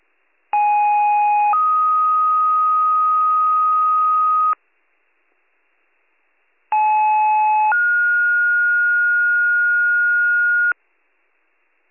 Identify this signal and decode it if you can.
Two-tone page, heard as a one-second tone, then a three-second tone, about 830 then 1250 Hz, about 860 then 1430 Hz. It is heard from 0.5 to 10.7 s.